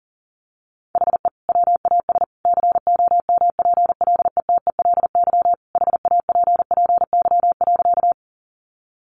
Morse code, heard as a keyed tone, about 40 wpm, 700 Hz. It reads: HE WAS COMPLETELY HAPPY.
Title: Sentences from Top 1100 Words 40wpm